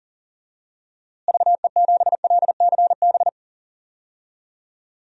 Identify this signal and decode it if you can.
Morse code, heard as a keyed tone, about 40 wpm, 700 Hz. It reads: VE7LCB
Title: DXCC-100 Call Signs - 6 Characters - Repeated 3x - 40wpm